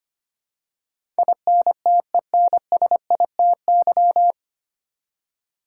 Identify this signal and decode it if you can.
Morse code, heard as a keyed tone, about 25 wpm, 700 Hz. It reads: INTENSITY
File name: Nine Letter Words - 25wpm